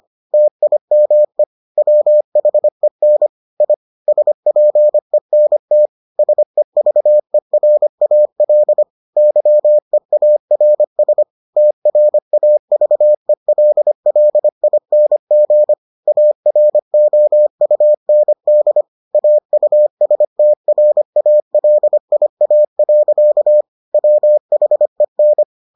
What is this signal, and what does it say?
Morse code, heard as a keyed tone, about 25 wpm, 600 Hz. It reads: TIME WHEN I SPENT SEVERAL YEARS TRAVELLING AROUND AUSTRALIA. WHEN